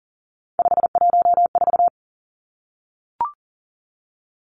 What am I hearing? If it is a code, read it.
Morse code, heard as a keyed tone, about 38 wpm, 700 Hz. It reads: 514
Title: Three Challenging Letter-Number Combinations 40wpm